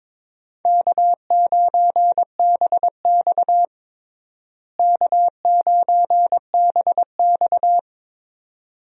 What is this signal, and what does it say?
Morse code, heard as a keyed tone, about 22 wpm, 700 Hz. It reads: K9BX K9BX
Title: USA Call Signs - 4 Characters - Repeated 2x - 22wpm